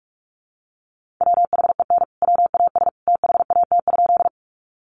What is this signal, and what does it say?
Morse code, heard as a keyed tone, about 44 wpm, 700 Hz. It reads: WHEN WAS THAT?
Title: Sentences from Top 100 Words 45wpm